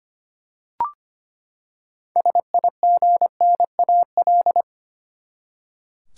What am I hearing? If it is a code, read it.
Morse code, heard as a keyed tone, about 25 wpm, 700 Hz. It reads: SIGNAL
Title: Six Letter Words - 25wpm